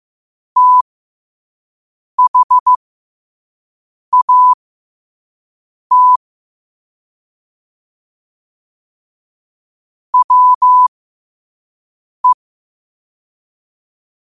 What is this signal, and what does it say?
Morse code, heard as a keyed tone, about 15 wpm, spaced as for 5 wpm, 1000 Hz. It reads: THAT WE
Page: New quotes every day in morse code at 5 Words per minute.